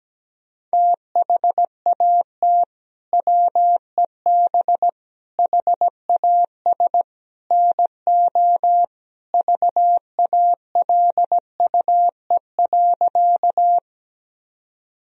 Morse code, heard as a keyed tone, about 17 wpm, 700 Hz. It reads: THAT WEB HAS NO VALUE.
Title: Sentences from Top 1000 Words - Review Part 4 - 17wpm